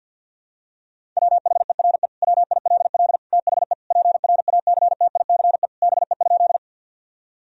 Morse code, heard as a keyed tone, about 50 wpm, 700 Hz. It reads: WHERE WILL THE PRACTICE BE?
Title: Sentences from Top 1000 Words - Review Part 4 - 50wpm